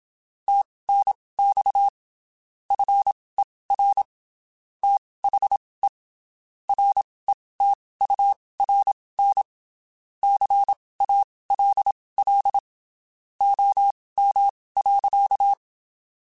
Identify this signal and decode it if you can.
Morse code, heard as a keyed tone, about 27 wpm, spaced as for 19 wpm, 800 Hz. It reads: TNX FER THE RETURN CALL OM.
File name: (spaced) / QSOs Spaced-mp3s / kc7haa @25 sp